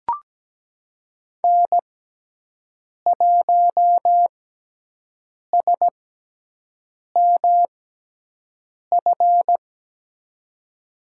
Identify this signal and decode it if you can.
Morse code, heard as a keyed tone, about 17 wpm, spaced as for 6 wpm, 700 Hz. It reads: N1SMF
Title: USA Call Signs - 5 Characters - 6x Character Spacing - 17wpm